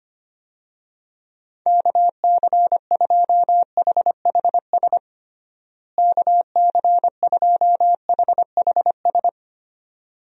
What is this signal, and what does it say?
Morse code, heard as a keyed tone, about 25 wpm, 700 Hz. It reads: KC2HHS KC2HHS
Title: USA Call Signs - 6 Characters - Repeated 2x - 25wpm